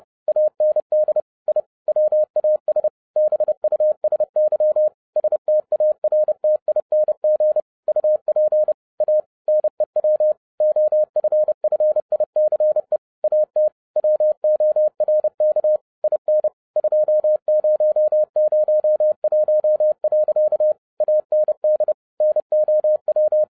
Morse code, heard as a keyed tone, about 30 wpm, 610 Hz. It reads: AND I WAS BUSY STARTING UP A NEW OFFICE AT WORK IN 2001. AND NOW
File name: 2012-01-mp3 / n11-I DIDNT MISS THE SOLAR CYCLE 24-30wpm